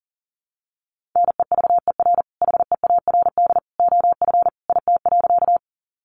Morse code, heard as a keyed tone, about 40 wpm, 700 Hz. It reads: NEVER HEARD OF IT.
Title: Sentences from Top 400 Words 40wpm